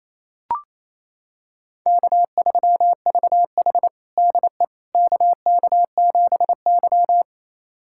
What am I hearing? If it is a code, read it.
Morse code, heard as a keyed tone, about 27 wpm, 700 Hz. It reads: K3VH DE KK7Y
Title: USA Callsign DE Callsign - 4 Characters - 28wpm